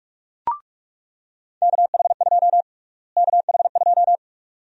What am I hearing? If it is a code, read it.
Morse code, heard as a keyed tone, about 46 wpm, 700 Hz. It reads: KH2 KH2
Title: USA Call Sign Prefixes - Repeated 2x - 45wpm